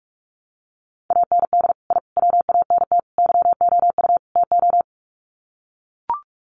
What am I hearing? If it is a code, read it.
Morse code, heard as a keyed tone, about 44 wpm, 700 Hz. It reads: AND I WANT YOU TO
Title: Sets of 5 Words Encore 45wpm